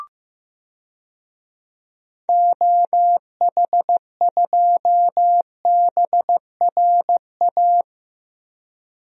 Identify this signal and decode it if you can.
Morse code, heard as a keyed tone, about 15 wpm, 700 Hz. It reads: OH2BRA
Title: DXCC-100 Call Signs - 6 Characters - Repeated 2x - 15wpm